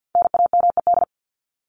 Morse code, heard as a keyed tone, about 50 wpm, 700 Hz. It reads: NAMED
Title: ED Suffix - Top 100 Words with Base Words 50wpm